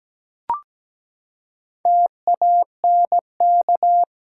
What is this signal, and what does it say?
Morse code, heard as a keyed tone, about 17 wpm, 700 Hz. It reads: TANK
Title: Four Letter Words - 17wpm